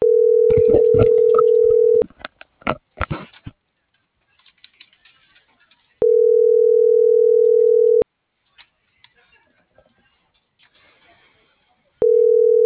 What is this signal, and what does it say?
Ringback tone (North America), 440 + 480 Hz.